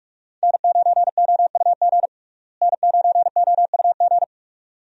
Morse code, heard as a keyed tone, about 47 wpm, 700 Hz. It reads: N9OUG N9OUG
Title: USA Call Signs - 5 Characters - Repeated 2x - 45wpm